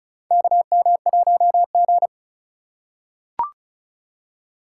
Morse code, heard as a keyed tone, about 34 wpm, 700 Hz. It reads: KM1G
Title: USA Call Signs - 4 Characters - 4x Character Spacing - 35wpm